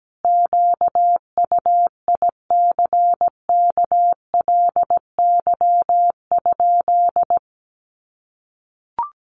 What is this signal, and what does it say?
Morse code, heard as a keyed tone, about 17 wpm, 700 Hz. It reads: QUICKLY?
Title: Sentences from Top 700 Words 17wpm